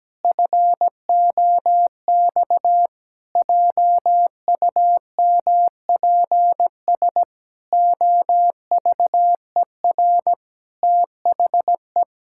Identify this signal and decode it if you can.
Morse code, heard as a keyed tone, about 17 wpm, 700 Hz. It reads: FOX JUMPS OVER THE
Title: Pangram Challenge v2 - 17wpm